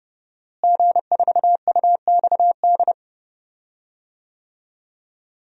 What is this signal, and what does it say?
Morse code, heard as a keyed tone, about 32 wpm, 700 Hz. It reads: G4UXD
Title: DXCC-100 Call Signs - 5 Characters - Repeated 2x - 30wpm